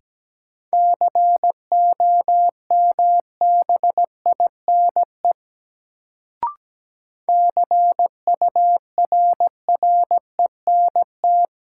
Morse code, heard as a keyed tone, about 17 wpm, 700 Hz. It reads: COMBINE CURRENT
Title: Sentences from Top 1600 Words - New Word Drill Alt - 17wpm